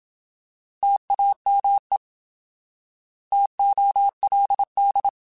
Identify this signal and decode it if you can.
Morse code, heard as a keyed tone, about 27 wpm, 800 Hz. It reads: TAME TOLD
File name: Alphabetical / Alphabetical mp3s / Ts @25